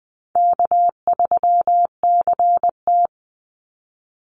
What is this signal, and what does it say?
Morse code, heard as a keyed tone, about 20 wpm, 700 Hz. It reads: K3CT
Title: Easy Call Signs Repeated 3x - 20wpm